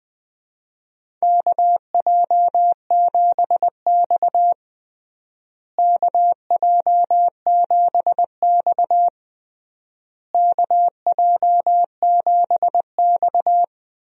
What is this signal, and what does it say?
Morse code, heard as a keyed tone, about 21 wpm, 700 Hz. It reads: KJ7X KJ7X KJ7X
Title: USA Call Signs - 4 Characters - Repeated 3x - 20wpm